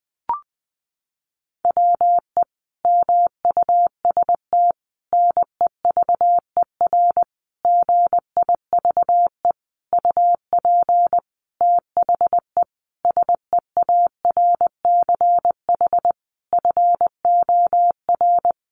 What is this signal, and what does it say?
Morse code, heard as a keyed tone, about 20 wpm, 700 Hz. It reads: WE MUST NEVER GIVE UP THE SEARCH FOR
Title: Sentences from Top 1500 Words 20wpm